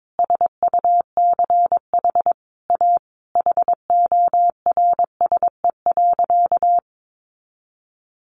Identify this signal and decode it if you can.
Morse code, heard as a keyed tone, about 22 wpm, 700 Hz. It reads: SUCH A HORSE.
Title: Sentences from Top 1100 Words 22wpm